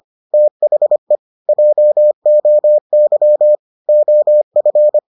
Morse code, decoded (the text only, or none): THE JOY OF